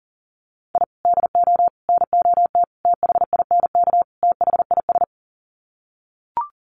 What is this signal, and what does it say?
Morse code, heard as a keyed tone, about 40 wpm, 700 Hz. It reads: I DO NOT THINK THIS
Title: Sets of 5 Words Encore 40wpm